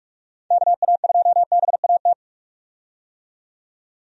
Morse code, heard as a keyed tone, about 43 wpm, 700 Hz. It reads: KA2BAT
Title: USA Call Signs - 6 Characters - 45wpm